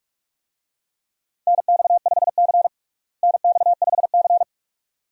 Morse code, heard as a keyed tone, about 41 wpm, 700 Hz. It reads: NX5C NX5C
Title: USA Call Signs - 4 Characters - Repeated 2x - 45wpm